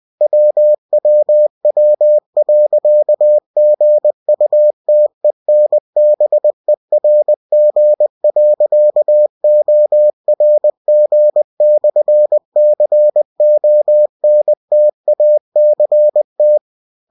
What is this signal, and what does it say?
Morse code, heard as a keyed tone, about 20 wpm, 600 Hz. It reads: WWW.GUTENBERG.ORG/CONTACT
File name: cw-books / winnie-the-pooh / winnie-20wpm-page-0121